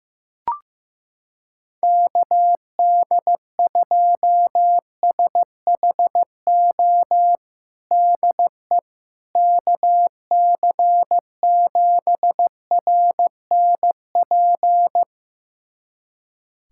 Morse code, heard as a keyed tone, about 16 wpm, 700 Hz. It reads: KD2SHO DE KC7RNP
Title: USA Callsign DE Callsign - 6 Characters - 15wpm